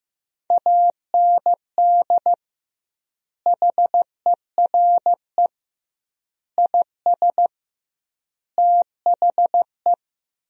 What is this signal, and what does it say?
Morse code, heard as a keyed tone, about 15 wpm, 700 Hz. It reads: AND HERE IS THE
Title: 2x Word-Spacing - Sentences from Top 500 Words - 15wpm